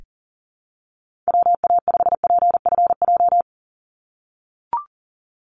Morse code, heard as a keyed tone, about 43 wpm, 700 Hz. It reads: WA5PFJ
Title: Call Signs - 40wpm